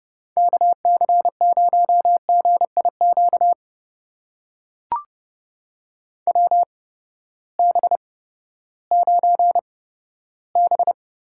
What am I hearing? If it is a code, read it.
Morse code, heard as a keyed tone, about 30 wpm, spaced as for 15 wpm, 700 Hz. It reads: KC0GIQ WB9B
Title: USA Call Signs - 6 Characters - 8x Character Spacing - 30wpm